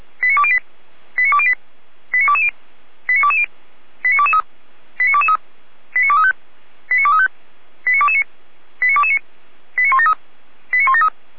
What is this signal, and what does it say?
Five-tone selective call (ZVEI), heard as quick runs of five tones, tones of about 70 ms. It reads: 89208 89208 89220 89220 89223 89223 89235 89235 89229 89229 89173 89173